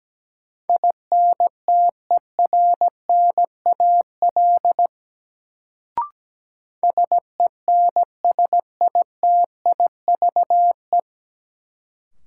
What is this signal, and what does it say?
Morse code, heard as a keyed tone, about 17 wpm, 700 Hz. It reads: INTERNAL SENSITIVE
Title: Sentences from Top 1400 Words - New Word Drill - 17wpm